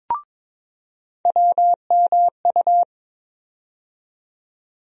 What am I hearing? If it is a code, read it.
Morse code, heard as a keyed tone, about 22 wpm, 700 Hz. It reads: WMU
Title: USA Call Sign Postfixes - 3 Letters 22wpm